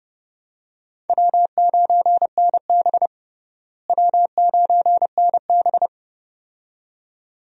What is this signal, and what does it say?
Morse code, heard as a keyed tone, about 30 wpm, 700 Hz. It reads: W9NB W9NB
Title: USA Call Signs - 4 Characters - Repeated 2x - 30wpm